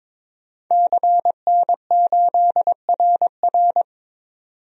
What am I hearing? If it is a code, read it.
Morse code, heard as a keyed tone, about 22 wpm, 700 Hz. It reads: CN8RR
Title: DXCC-100 Call Signs - 5 Characters - 22wpm